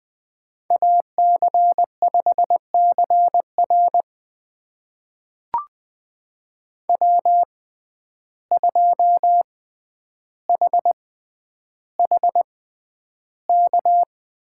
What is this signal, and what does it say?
Morse code, heard as a keyed tone, about 20 wpm, spaced as for 10 wpm, 700 Hz. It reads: AC5CR W2HHK
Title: USA Call Signs - 5 Characters - 6x Character Spacing - 20wpm